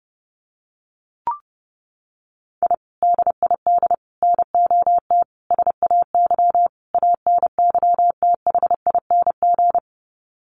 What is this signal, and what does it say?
Morse code, heard as a keyed tone, about 30 wpm, 700 Hz. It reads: I DID NOT SAY ANYTHING
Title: Sets of 5 Words Encore2 30wpm